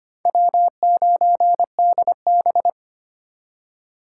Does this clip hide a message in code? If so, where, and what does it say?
Yes: Morse code from 0.3 to 2.7 s, heard as a keyed tone, about 33 wpm, 700 Hz. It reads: W9DB